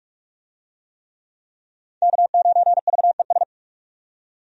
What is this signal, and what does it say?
Morse code, heard as a keyed tone, about 46 wpm, 700 Hz. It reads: K9VES
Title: USA Call Signs - 5 Characters - 45wpm